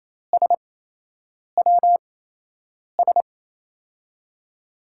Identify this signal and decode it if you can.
Morse code, heard as a keyed tone, about 28 wpm, spaced as for 8 wpm, 700 Hz. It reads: SWS
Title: USA Call Signs - 5 Characters - 8x Character Spacing - 28wpm